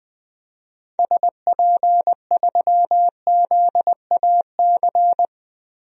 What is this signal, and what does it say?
Morse code, heard as a keyed tone, about 21 wpm, 700 Hz. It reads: SP3ZAC
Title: DXCC-100 Call Signs - 6 Characters - Repeated 3x - 20wpm